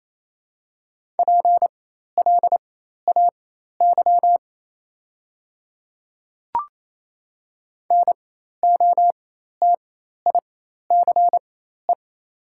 Morse code, heard as a keyed tone, about 27 wpm, spaced as for 13 wpm, 700 Hz. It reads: PLAY NOTICE
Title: 4x Character Spacing - Top 500 Words - 28wpm